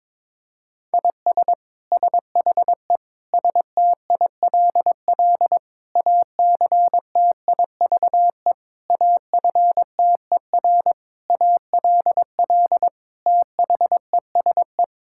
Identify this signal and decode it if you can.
Morse code, heard as a keyed tone, about 22 wpm, 700 Hz. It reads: IS SHE STILL ACTIVE AFTER ALL THESE
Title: Sentences from Top 1000 Words - Review Part 6 - 22wpm